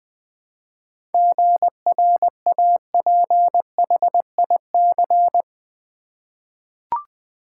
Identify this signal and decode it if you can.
Morse code, heard as a keyed tone, about 20 wpm, 700 Hz. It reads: GRAPHIC
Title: 7 Letter Words - First 3 Letters Given - 20wpm